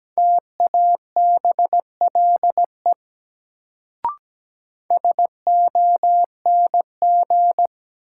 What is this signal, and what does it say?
Morse code, heard as a keyed tone, about 17 wpm, 700 Hz. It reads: TABLE SONG
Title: Sentences from Top 400 Words - New Word Drill - 17wpm